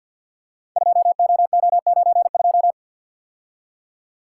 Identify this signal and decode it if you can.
Morse code, heard as a keyed tone, about 48 wpm, 700 Hz. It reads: 2OO92
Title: Code Groups - 5 Letter:Numbers - 50wpm